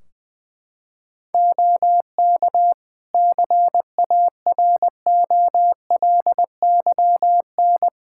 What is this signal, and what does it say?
Morse code, heard as a keyed tone, about 20 wpm, 700 Hz. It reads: OK CAROLYN
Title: Practice QSO - Exchange 4 - Ending - 20wpm